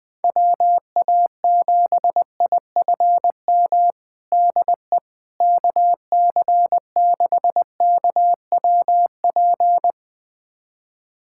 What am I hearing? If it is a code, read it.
Morse code, heard as a keyed tone, about 20 wpm, 700 Hz. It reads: WA7IFM DE KC6KWP
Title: USA Callsign DE Callsign - 6 Characters - 20wpm